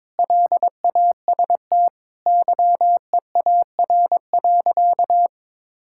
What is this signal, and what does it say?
Morse code, heard as a keyed tone, about 22 wpm, 700 Hz. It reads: LAST YEAR.
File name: Sentences with 7-Word Phrases - 22wpm